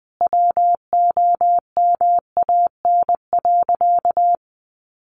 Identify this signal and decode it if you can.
Morse code, heard as a keyed tone, about 19 wpm, 700 Hz. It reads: WOMAN.
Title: Sentences from Top 1100 Words 20wpm